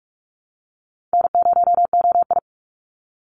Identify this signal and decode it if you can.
Morse code, heard as a keyed tone, about 46 wpm, 700 Hz. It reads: N0OI
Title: NA SOTA Chaser Callsigns 45wpm